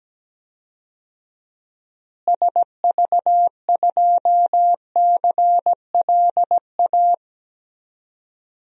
Morse code, heard as a keyed tone, about 17 wpm, 700 Hz. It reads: SV2CLA
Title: DXCC-100 Call Signs - 6 Characters - Repeated 2x - 17wpm